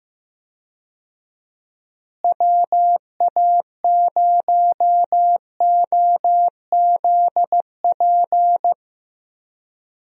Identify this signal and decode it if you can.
Morse code, heard as a keyed tone, about 15 wpm, 700 Hz. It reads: WA0OZP
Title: USA Call Signs - 6 Characters - Repeated 2x - 15wpm